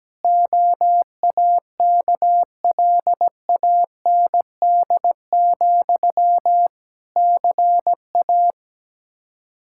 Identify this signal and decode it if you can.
Morse code, heard as a keyed tone, about 17 wpm, 700 Hz. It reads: OAKLAND, CA
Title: K1USN SST QSO Practice 17wpm